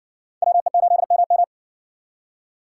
Morse code, heard as a keyed tone, about 58 wpm, 700 Hz. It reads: WE8MM